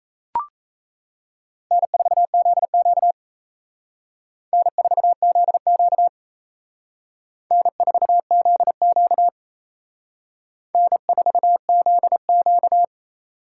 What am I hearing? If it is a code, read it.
Morse code, heard as a keyed tone, about 34 wpm, 700 Hz. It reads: N4ZQ N4ZQ N4ZQ N4ZQ
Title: Speed Racing - USA 1x2 Call Signs 28wpm